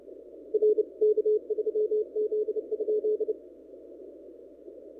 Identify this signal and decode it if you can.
Morse code, heard as a keyed tone, about 29 wpm, 430 Hz. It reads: RK3Z?
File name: n6tr / PT0S